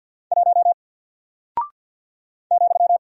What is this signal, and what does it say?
Morse code, heard as a keyed tone, about 48 wpm, 700 Hz. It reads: 1 ,
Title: Single Character (Letter:Number:Punctuation:Prosign) Alt 50wpm